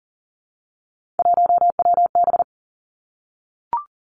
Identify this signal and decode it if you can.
Morse code, heard as a keyed tone, about 40 wpm, 700 Hz. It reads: 1WB
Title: Three Challenging Letter-Number Combinations 40wpm